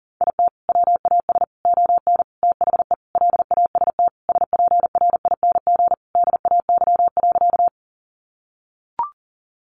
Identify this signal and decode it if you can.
Morse code, heard as a keyed tone, about 37 wpm, 700 Hz. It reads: IT WAS ON THE LAST SPRING DAY.
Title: Sentences from Top 1200 Words 40wpm